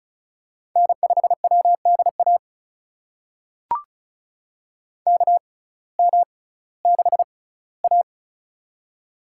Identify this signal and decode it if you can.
Morse code, heard as a keyed tone, about 36 wpm, spaced as for 19 wpm, 700 Hz. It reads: N5WDA KM6A
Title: USA Call Signs - 6x Character Spacing - 35wpm